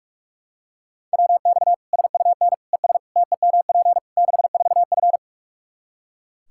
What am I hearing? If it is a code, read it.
Morse code, heard as a keyed tone, about 42 wpm, 700 Hz. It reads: WX SUN ES TEMP 64F
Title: Practice QSO - Exchange 2 - Info - 45wpm